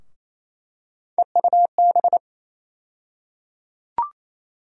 Morse code, heard as a keyed tone, about 28 wpm, 700 Hz. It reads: EUB
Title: USA Call Sign Postfixes - 3 Letters 28wpm